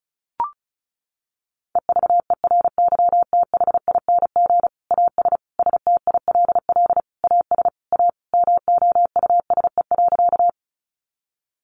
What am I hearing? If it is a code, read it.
Morse code, heard as a keyed tone, about 34 wpm, 700 Hz. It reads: EVERYTHING AS STILL AS A MOUSE.
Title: Sentences from Top 700 Words 35wpm